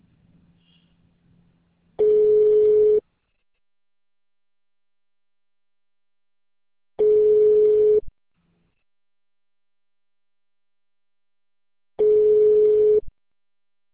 Ringback tone (Europe, 425 Hz).